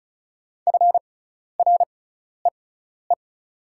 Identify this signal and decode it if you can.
Morse code, heard as a keyed tone, about 34 wpm, spaced as for 12 wpm, 690 Hz. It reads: FREE
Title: 6x Character Spacing - Top 500 Words - 35wpm